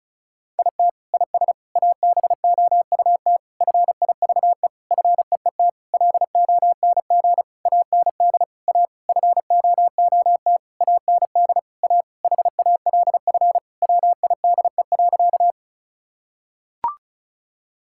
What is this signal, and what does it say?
Morse code, heard as a keyed tone, about 36 wpm, 700 Hz. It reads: IT IS ABOUT FIVE FEET LONG AND A FOOT AND A HALF WIDE.
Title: Sentences from Top 1000 Words - Review Part 6 - 35wpm